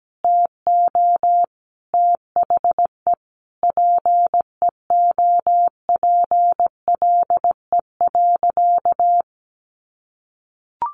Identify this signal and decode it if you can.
Morse code, heard as a keyed tone, about 17 wpm, 700 Hz. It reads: TO THE PEOPLE.
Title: Sentences from Top 1500 Words 17wpm